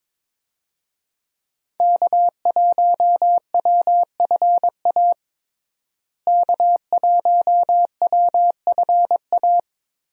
Morse code, heard as a keyed tone, about 22 wpm, 700 Hz. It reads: K1WFA K1WFA
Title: USA Call Signs - 5 Characters - Repeated 2x - 22wpm